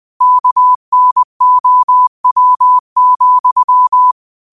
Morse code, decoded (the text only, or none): KNOW,